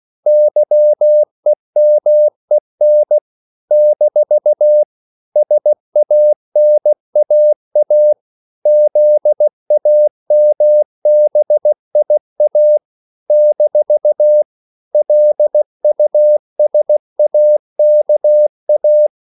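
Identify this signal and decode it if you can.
Morse code, heard as a keyed tone, about 16 wpm, 600 Hz. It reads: YEMEN - SANAA ZAMBIA - LUSAKA
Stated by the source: CW practice of national capitals at 16 words per minute, from ON6ZQ.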